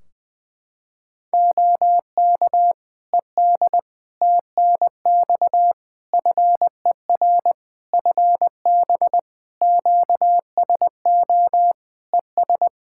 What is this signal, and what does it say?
Morse code, heard as a keyed tone, about 20 wpm, 700 Hz. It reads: OK ED TNX FER FB QSO ES
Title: Practice QSO - Exchange 4 - Ending - 20wpm